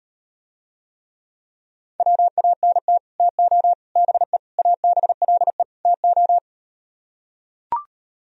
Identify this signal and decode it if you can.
Morse code, heard as a keyed tone, about 37 wpm, 700 Hz. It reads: WANT TO BE ABLE TO